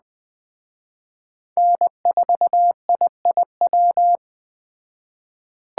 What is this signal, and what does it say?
Morse code, heard as a keyed tone, about 20 wpm, 700 Hz. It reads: N4IIW
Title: Speed Racing - USA 1x3 Call Signs 17wpm